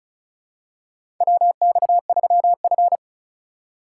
Morse code, heard as a keyed tone, about 36 wpm, 700 Hz. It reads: WX3F
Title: USA Call Signs - 2x Character Spacing - 35wpm